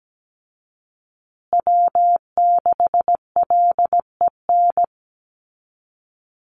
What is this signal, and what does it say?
Morse code, heard as a keyed tone, about 17 wpm, 700 Hz. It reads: W6LEN
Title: Call Signs Repeated 3x - 17wpm